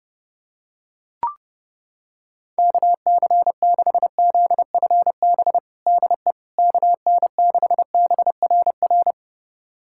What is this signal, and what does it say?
Morse code, heard as a keyed tone, about 30 wpm, 700 Hz. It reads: KC6ZFB DE KN6BRR
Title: USA Callsign DE Callsign 30wpm